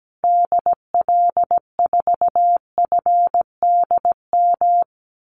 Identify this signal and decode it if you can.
Morse code, heard as a keyed tone, about 17 wpm, 700 Hz. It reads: DL4FDM
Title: Call Signs Repeated 2x - 17wpm